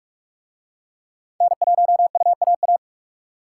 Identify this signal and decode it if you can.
Morse code, heard as a keyed tone, about 40 wpm, 700 Hz. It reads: N1UAA